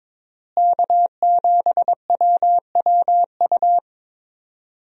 Morse code, heard as a keyed tone, about 22 wpm, 700 Hz. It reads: K7WWU
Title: USA Call Signs - 5 Characters - Repeated 2x - 22wpm